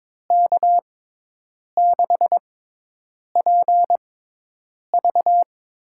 Morse code, heard as a keyed tone, about 22 wpm, spaced as for 8 wpm, 700 Hz. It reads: K6PV
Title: USA Call Signs - 6x Character Spacing - 22wpm